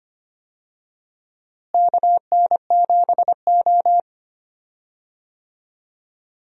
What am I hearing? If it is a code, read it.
Morse code, heard as a keyed tone, about 25 wpm, 700 Hz. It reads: KN7O
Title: USA Call Signs - 4 Characters - Repeated 2x - 25wpm